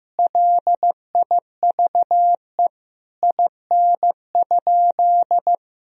Morse code, heard as a keyed tone, about 16 wpm, 700 Hz. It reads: LIVE IN?
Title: Sentences from Top 1000 Words - Review Part 2 - 15wpm